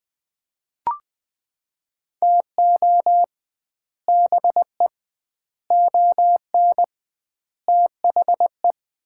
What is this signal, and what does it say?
Morse code, heard as a keyed tone, about 20 wpm, 700 Hz. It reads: TO BE ON THE
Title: 2x Word-Spacing - Sets of 4 Words - 20wpm